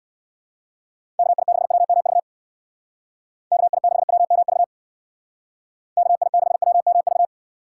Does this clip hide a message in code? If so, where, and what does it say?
Yes: Morse code from 1.2 to 7.3 s, heard as a keyed tone, about 66 wpm, 700 Hz. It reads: KI6WMV KI6WMV KI6WMV